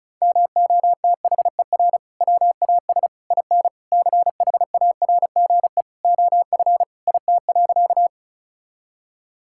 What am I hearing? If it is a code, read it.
Morse code, heard as a keyed tone, about 34 wpm, 700 Hz. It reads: MOTHER WAS IN CHARGE OF IT.
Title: Sentences from Top 1000 Words - Review Part 6 - 35wpm